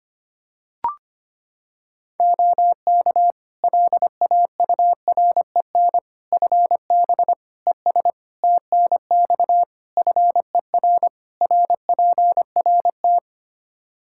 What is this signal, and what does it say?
Morse code, heard as a keyed tone, about 25 wpm, 700 Hz. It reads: OK LAUREN FB ES TNX FER RPRT
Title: Practice QSO - Exchange 2 - Info - 25wpm